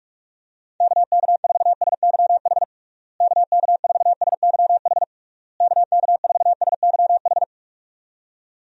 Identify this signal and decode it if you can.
Morse code, heard as a keyed tone, about 46 wpm, 700 Hz. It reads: KK4SYH KK4SYH KK4SYH
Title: USA Call Signs - 6 Characters - Repeated 3x - 45wpm